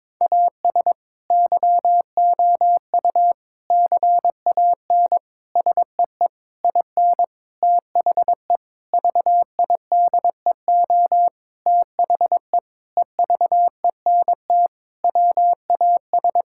Morse code, heard as a keyed tone, about 22 wpm, 700 Hz. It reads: AS YOU CAN SEE IN THE VIDEO THE EVENT WAS
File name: Sentences with 7-Word Phrases - 22wpm